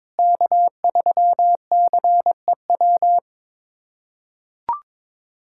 Morse code, heard as a keyed tone, about 22 wpm, 700 Hz. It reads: K3CEW
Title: USA Callsign DE Callsign 22wpm